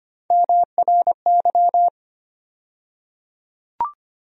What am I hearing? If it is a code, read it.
Morse code, heard as a keyed tone, about 25 wpm, 700 Hz. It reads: MRY
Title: USA Call Sign Postfixes - 3 Letters 25wpm